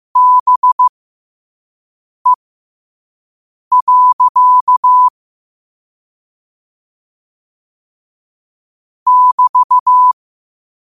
Morse code, heard as a keyed tone, about 15 wpm, spaced as for 5 wpm, 1000 Hz. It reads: BE. <BT>
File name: qotdfiles / 5wpm